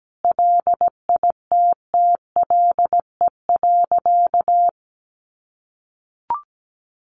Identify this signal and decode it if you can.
Morse code, heard as a keyed tone, about 17 wpm, 700 Hz. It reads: LITTLE.
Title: Sentences from Top 1500 Words 17wpm